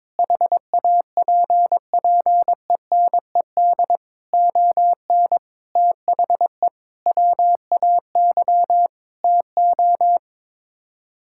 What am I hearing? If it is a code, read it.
Morse code, heard as a keyed tone, about 22 wpm, 700 Hz. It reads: HAPPENED ON THE WAY TO
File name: practice / 22wpm / Sets of 8 Words - 22wpm